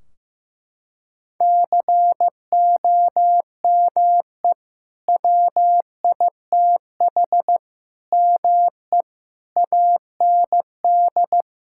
Morse code, heard as a keyed tone, about 15 wpm, 700 Hz. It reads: COME WITH ME AND
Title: Sentences from Top 1600 Words 15wpm